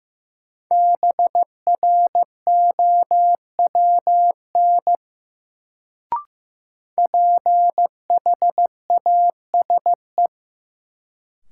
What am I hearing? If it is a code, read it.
Morse code, heard as a keyed tone, about 15 wpm, 700 Hz. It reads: BROWN PHASE
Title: Sentences from Top 1200 Words - New Word Drill - 15wpm